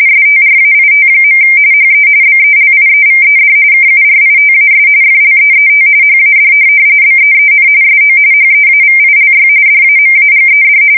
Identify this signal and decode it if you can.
RTTY, about 45.45 baud. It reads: STRAIT. HE BELIEVED KURITAS FLEET HAD BEEN SO HEAVILY DAMAGED
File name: fsk170 / USS-New-Jersey